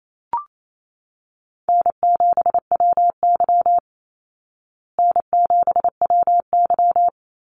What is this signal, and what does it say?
Morse code, heard as a keyed tone, about 28 wpm, 700 Hz. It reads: N7WY N7WY
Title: Call Signs Repeated 2x - 28wpm